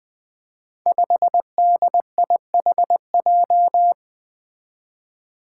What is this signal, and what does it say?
Morse code, heard as a keyed tone, about 20 wpm, 700 Hz. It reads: 5DIHJ
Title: Five Challenging Letter-Number Combinations 20wpm